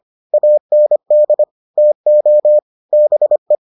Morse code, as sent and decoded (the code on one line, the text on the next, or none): .- -. -.. / - --- / -... .
AND TO BE